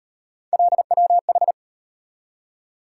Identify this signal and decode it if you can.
Morse code, heard as a keyed tone, about 38 wpm, 700 Hz. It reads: LWH